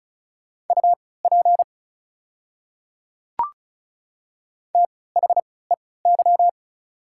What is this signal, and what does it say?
Morse code, heard as a keyed tone, about 36 wpm, spaced as for 20 wpm, 700 Hz. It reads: UP THEY
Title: 3x Character Spacing - Top 100 Words - 35wpm